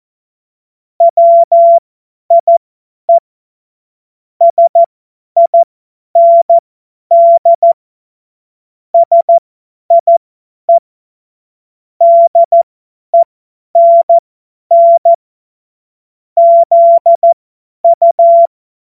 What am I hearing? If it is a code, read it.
Morse code, heard as a keyed tone, about 14 wpm, spaced as for 10 wpm, 680 Hz. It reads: WIE SIND SIE DENN ZU
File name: Morsekurs / Witze / Witz 14.November 2023 50 F70